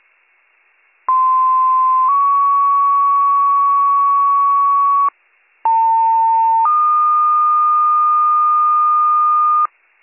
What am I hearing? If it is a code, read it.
Two-tone page, heard as a one-second tone, then a three-second tone, about 1040 then 1120 Hz, about 900 then 1220 Hz. It is heard from 1.1 to 9.7 s.